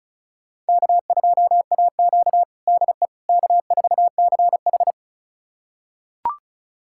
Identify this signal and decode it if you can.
Morse code, heard as a keyed tone, about 34 wpm, 700 Hz. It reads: K2AQ DE K4CH
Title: USA Callsign DE Callsign - 4 Characters - 35wpm